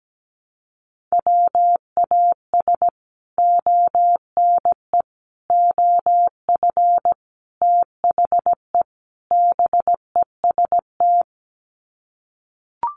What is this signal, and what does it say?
Morse code, heard as a keyed tone, about 17 wpm, 700 Hz. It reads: WAS ONE OF THE BEST
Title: Sets of 5 Words Encore2 17wpm